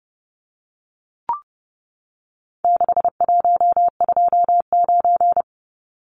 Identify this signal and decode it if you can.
Morse code, heard as a keyed tone, about 30 wpm, 700 Hz. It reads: 6129
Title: Numbers - 4-Digit 30wpm